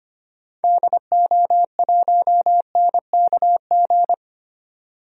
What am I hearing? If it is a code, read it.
Morse code, heard as a keyed tone, about 25 wpm, 700 Hz. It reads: DO1NKG
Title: DXCC-100 Call Signs - 6 Characters - 25wpm